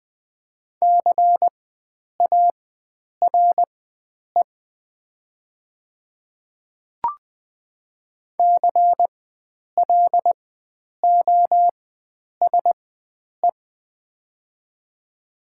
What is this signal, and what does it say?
Morse code, heard as a keyed tone, about 20 wpm, spaced as for 9 wpm, 700 Hz. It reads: CARE CLOSE
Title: 4x Character Spacing - Top 300 Words - 20wpm